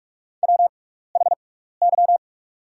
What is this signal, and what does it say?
Morse code, heard as a keyed tone, about 44 wpm, spaced as for 15 wpm, 700 Hz. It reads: WHY
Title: 6x Character Spacing - Top 300 Words - 45wpm